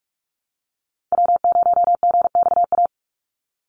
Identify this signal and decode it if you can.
Morse code, heard as a keyed tone, about 44 wpm, 700 Hz. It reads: W0GXA
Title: Call Signs Repeated 2x - 45wpm